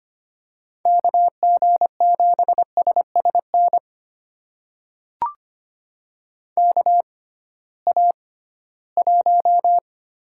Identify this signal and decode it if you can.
Morse code, heard as a keyed tone, about 25 wpm, spaced as for 16 wpm, 700 Hz. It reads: KG7SSN KA1
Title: USA Call Signs - 6x Character Spacing - 25wpm